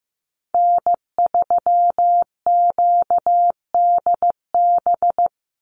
Morse code, heard as a keyed tone, about 15 wpm, 700 Hz. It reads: N3QDB
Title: USA Call Signs 15wpm